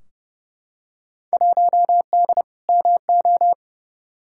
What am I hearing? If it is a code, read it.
Morse code, heard as a keyed tone, about 29 wpm, 700 Hz. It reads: 1D MO
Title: Field Day Exchanges - 30wpm